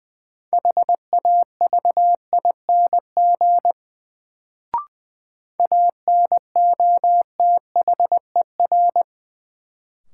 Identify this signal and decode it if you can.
Morse code, heard as a keyed tone, about 20 wpm, 700 Hz. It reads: HAVING ANOTHER
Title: Sets of 5 Words Encore - New Word Drill - 20wpm